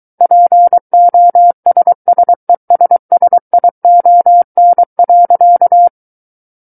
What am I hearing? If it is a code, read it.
Morse code, heard as a keyed tone, about 23 wpm, 700 Hz. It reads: POSSESSION.
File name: Book One / 14-Chapter 14 - 23 WPM